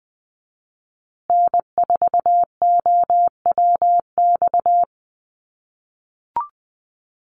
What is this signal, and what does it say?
Morse code, heard as a keyed tone, about 20 wpm, 700 Hz. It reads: N4OWX
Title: USA Call Signs Repeated 3x 20wpm